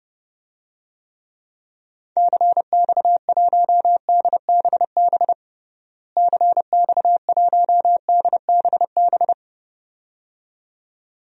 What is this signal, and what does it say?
Morse code, heard as a keyed tone, about 30 wpm, 700 Hz. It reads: CX1DBB CX1DBB
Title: DXCC-100 Call Signs - 6 Characters - Repeated 2x - 30wpm